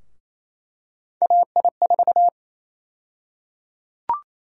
Morse code, heard as a keyed tone, about 28 wpm, 700 Hz. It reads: AI4
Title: USA Call Sign Prefixes - Repeated 2x - 28wpm